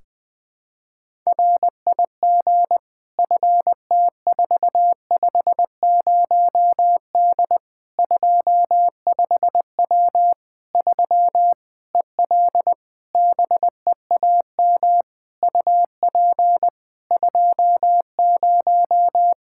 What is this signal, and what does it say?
Morse code, heard as a keyed tone, about 20 wpm, 700 Hz. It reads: RIG FT450D 25W 3 EL BEAM UP 20
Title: Practice QSO - Exchange 2 - Info - 20wpm